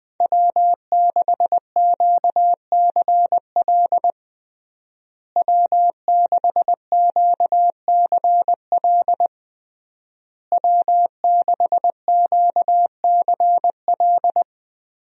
Morse code, heard as a keyed tone, about 20 wpm, 700 Hz. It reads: W6QCL W6QCL W6QCL
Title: USA Call Signs - 5 Characters - Repeated 3x - 20wpm